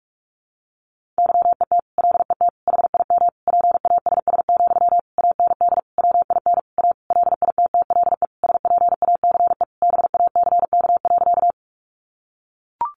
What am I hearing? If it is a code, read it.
Morse code, heard as a keyed tone, about 44 wpm, 700 Hz. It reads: YET LET HIM PASS, AND WIN A LITTLE SPACE BACK.
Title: Sentences from Top 1500 Words 45wpm